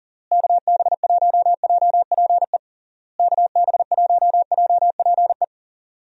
Morse code, heard as a keyed tone, about 40 wpm, 700 Hz. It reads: KB1JPE KB1JPE
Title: USA Call Signs - 6 Characters - Repeated 3x - 40wpm